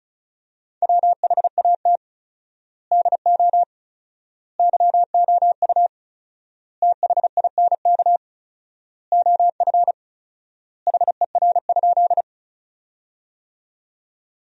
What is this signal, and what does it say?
Morse code, heard as a keyed tone, about 36 wpm, 700 Hz. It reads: WHAT DO YOU THINK OF HER?
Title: 4x Word-Spacing - Sentences from Top 200 Words - 35wpm